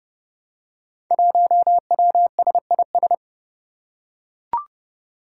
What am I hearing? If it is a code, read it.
Morse code, heard as a keyed tone, about 28 wpm, 700 Hz. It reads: 1WSIS
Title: Five Challenging Letter-Number Combinations 30wpm